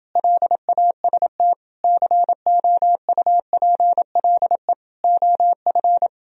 Morse code, heard as a keyed tone, about 29 wpm, 700 Hz. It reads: LAST COUPLE OF